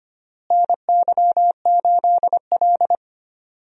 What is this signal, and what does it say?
Morse code, heard as a keyed tone, about 25 wpm, 700 Hz. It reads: NY8L
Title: USA Call Signs - 4 Characters - Repeated 2x - 25wpm